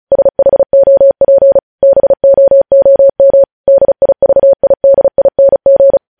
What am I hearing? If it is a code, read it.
Morse code, heard as a keyed tone, about 36 wpm, 550 Hz. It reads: SHOP BOOM DIVIDING
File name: podcast / News headlines-1x-35x35